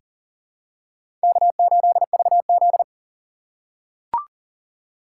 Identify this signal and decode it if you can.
Morse code, heard as a keyed tone, about 40 wpm, 700 Hz. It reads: K8VZ
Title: USA Call Signs - 2x Character Spacing - 40wpm